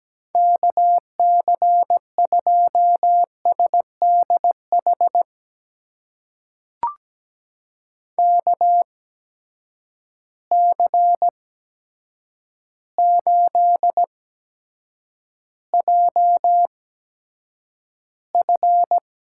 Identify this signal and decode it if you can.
Morse code, heard as a keyed tone, about 17 wpm, spaced as for 8 wpm, 700 Hz. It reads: KC2SDH KC8JF
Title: USA Call Signs - 8x Character Spacing - 17wpm